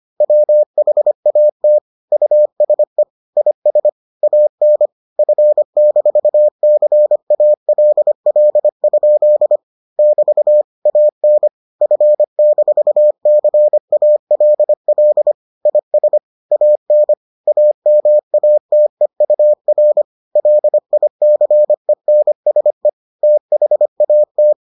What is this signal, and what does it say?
Morse code, heard as a keyed tone, about 25 wpm, 600 Hz. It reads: WHAT USE IS AN F-CALL? <BT> AN F-CALL IS AN AMATEUR LICENSE THAT